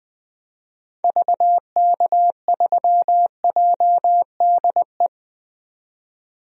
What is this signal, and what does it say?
Morse code, heard as a keyed tone, about 20 wpm, 700 Hz. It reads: VK3JDE
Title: DXCC-100 Call Signs - 6 Characters - 20wpm